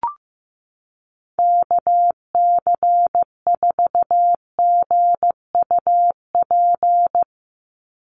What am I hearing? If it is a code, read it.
Morse code, heard as a keyed tone, about 15 wpm, 700 Hz. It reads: KC4GUP
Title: USA Call Signs 15wpm